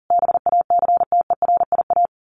Morse code, heard as a keyed tone, about 40 wpm, 700 Hz. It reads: BACTERIA
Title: A Sound Of Thunder 40wpm